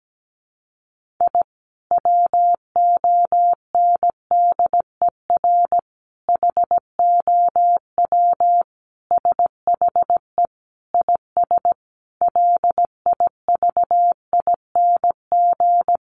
Morse code, heard as a keyed tone, about 17 wpm, 700 Hz. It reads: I WONDER HOW SHE IS LIVING
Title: Sentences from Top 1500 Words 17wpm